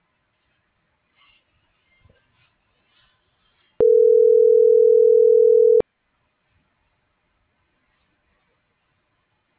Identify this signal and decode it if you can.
Ringback tone (North America), 440 + 480 Hz.